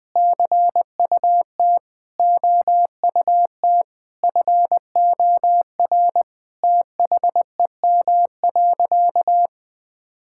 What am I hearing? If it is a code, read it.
Morse code, heard as a keyed tone, about 20 wpm, 700 Hz. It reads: CUT OUT FOR THEM.
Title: Sentences with 7-Word Phrases - Spoken Beginning - 20wpm